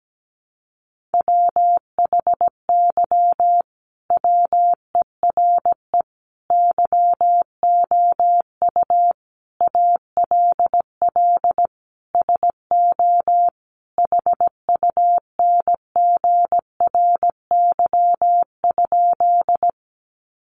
Morse code, heard as a keyed tone, about 17 wpm, 700 Hz. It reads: WHY WERE YOU ALL SO HUNGRY?
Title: Sentences from Top 1400 Words 17wpm